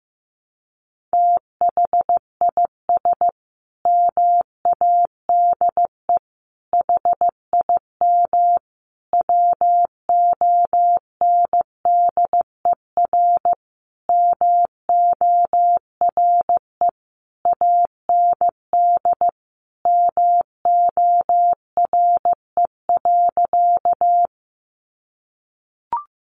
Morse code, heard as a keyed tone, about 15 wpm, 700 Hz. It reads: THIS MADE HIM WONDER MORE AND MORE.
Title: Sentences from Top 500 Words 15wpm